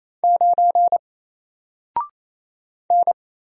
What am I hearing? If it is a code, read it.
Morse code, heard as a keyed tone, about 27 wpm, 700 Hz. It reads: 9 N
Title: Single Character (Letter:Number:Punctuation:Prosign) 28wpm